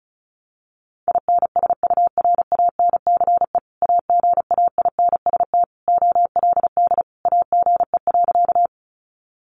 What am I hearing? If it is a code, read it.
Morse code, heard as a keyed tone, about 34 wpm, 700 Hz. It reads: INSURANCE AGAINST OLD AGE.
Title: Sentences from Top 1100 Words 35wpm